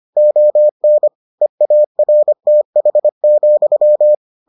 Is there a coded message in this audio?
Yes: Morse code from 0.2 to 4.2 s, heard as a keyed tone, about 25 wpm, 600 Hz. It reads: ON EARTH,